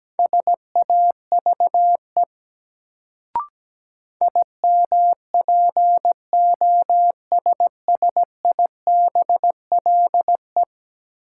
Morse code, heard as a keyed tone, about 17 wpm, 700 Hz. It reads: SAVE IMPOSSIBLE
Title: Sentences from Top 1100 Words - New Word Drill - 17wpm